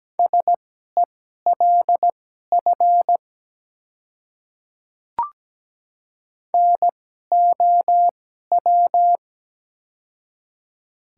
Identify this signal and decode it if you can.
Morse code, heard as a keyed tone, about 17 wpm, spaced as for 12 wpm, 690 Hz. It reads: SELF NOW
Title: 2x Character Spacing - Top 300 Words - 17wpm